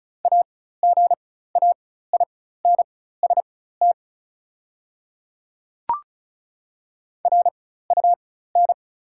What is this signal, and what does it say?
Morse code, heard as a keyed tone, about 36 wpm, spaced as for 16 wpm, 690 Hz. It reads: AGAINST RUN
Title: 4x Character Spacing - Top 500 Words - 35wpm